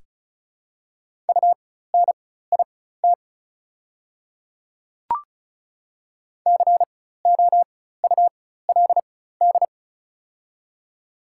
Morse code, heard as a keyed tone, about 36 wpm, spaced as for 16 wpm, 700 Hz. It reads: UNIT COULD
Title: 4x Character Spacing - Top 500 Words - 35wpm